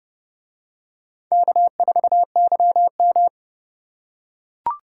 Morse code, heard as a keyed tone, about 30 wpm, 700 Hz. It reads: K4YM